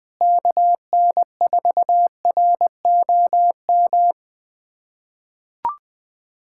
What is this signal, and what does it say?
Morse code, heard as a keyed tone, about 20 wpm, 700 Hz. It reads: KN4ROM
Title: USA Callsign DE Callsign 20wpm